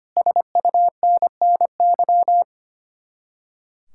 Morse code, heard as a keyed tone, about 25 wpm, 700 Hz. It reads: SUNNY
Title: Speed-Racing Common QSO Phrases - 25wpm